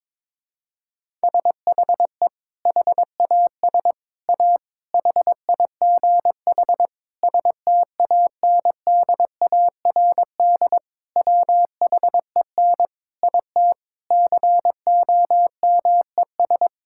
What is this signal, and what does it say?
Morse code, heard as a keyed tone, about 22 wpm, 700 Hz. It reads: SHE HAS A HIGH STANDARD WHEN IT COMES
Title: Sentences from Top 1000 Words - Review Part 6 - 22wpm